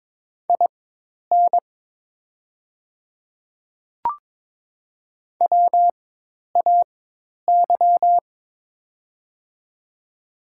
Morse code, heard as a keyed tone, about 23 wpm, spaced as for 10 wpm, 700 Hz. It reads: IN WAY
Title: 4x Character Spacing - Top 100 Words - 22wpm